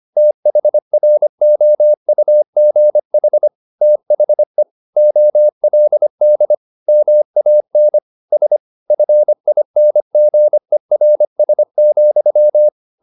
Morse code, heard as a keyed tone, about 25 wpm, 600 Hz. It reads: THROUGH THE OLD MAN S FINGERS,
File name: the old man and the sea part 1 25wpm